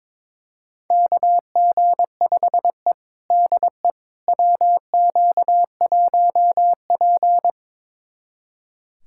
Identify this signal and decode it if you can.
Morse code, heard as a keyed tone, about 22 wpm, 700 Hz. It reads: KG5E DE WQ1P
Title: USA Callsign DE Callsign - 4 Characters - 22wpm